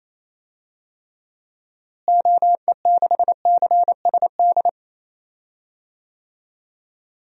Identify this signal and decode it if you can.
Morse code, heard as a keyed tone, about 28 wpm, 700 Hz. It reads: OE6CSD
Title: DXCC-100 Call Signs - 6 Characters - Repeated 2x - 28wpm